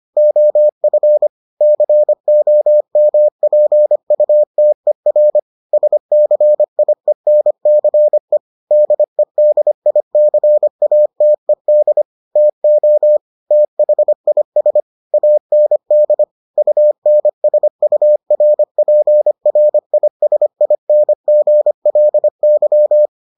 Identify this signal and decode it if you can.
Morse code, heard as a keyed tone, about 25 wpm, 600 Hz. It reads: OF COMPUTER SCIENCE DEDICATED TO THIS AND UNSURPRISINGLY